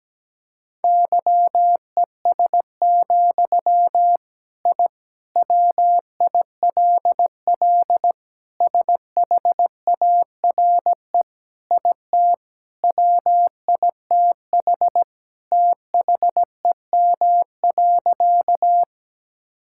Morse code, heard as a keyed tone, about 17 wpm, 700 Hz. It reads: YES, I WILL SHARE IT WITH THEM.
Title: Sentences from Top 1000 Words - Review Part 6 - 17wpm